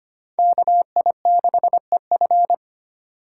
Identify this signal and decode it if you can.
Morse code, heard as a keyed tone, about 32 wpm, 700 Hz. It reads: KI6EF